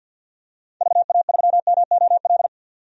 Morse code, heard as a keyed tone, about 49 wpm, 700 Hz. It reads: VA3MOL
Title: DXCC-100 Call Signs - 6 Characters - 50wpm